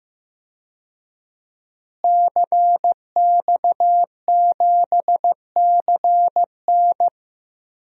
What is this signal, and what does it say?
Morse code, heard as a keyed tone, about 15 wpm, 700 Hz. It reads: CX7CN
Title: DXCC-100 Call Signs - 5 Characters - Repeated 2x - 15wpm